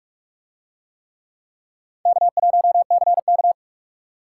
Morse code, heard as a keyed tone, about 47 wpm, 700 Hz. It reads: K1CK